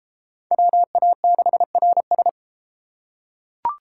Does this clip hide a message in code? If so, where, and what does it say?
Yes: Morse code from 0.5 to 2.3 s, heard as a keyed tone, about 30 wpm, 700 Hz. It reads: WA6RS